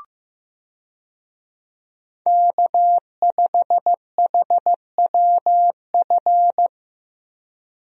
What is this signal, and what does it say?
Morse code, heard as a keyed tone, about 15 wpm, 700 Hz. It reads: K5HWF
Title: USA Call Signs - 5 Characters - Repeated 3x - 15wpm